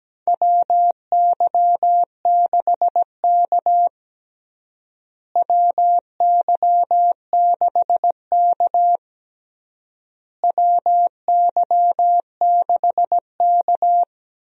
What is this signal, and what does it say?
Morse code, heard as a keyed tone, about 17 wpm, 700 Hz. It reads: WY6K WY6K WY6K
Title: USA Call Signs - 4 Characters - Repeated 3x - 17wpm